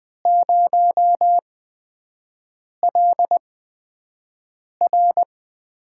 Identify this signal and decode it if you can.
Morse code, heard as a keyed tone, about 21 wpm, spaced as for 5 wpm, 700 Hz. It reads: 0LR
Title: USA Call Signs - 5 Characters - 8x Character Spacing - 20wpm